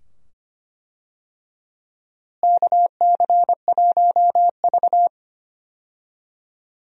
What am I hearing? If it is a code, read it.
Morse code, heard as a keyed tone, about 25 wpm, 700 Hz. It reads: KC1V
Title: USA Call Signs - 4 Characters - Repeated 2x - 25wpm